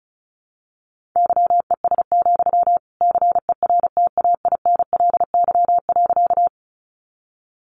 Morse code, heard as a keyed tone, about 34 wpm, 700 Hz. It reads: YES, CERTAINLY.
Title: Sentences from Top 900 Words 35wpm